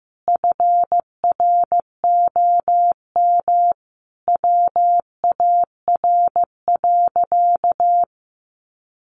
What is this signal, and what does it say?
Morse code, heard as a keyed tone, about 15 wpm, 700 Hz. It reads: FROM WAR.
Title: Sentences from Top 1500 Words 15wpm